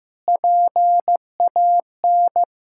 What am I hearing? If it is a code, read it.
Morse code, heard as a keyed tone, about 15 wpm, 700 Hz. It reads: PAN
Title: Pangram Challenge v2 - 15wpm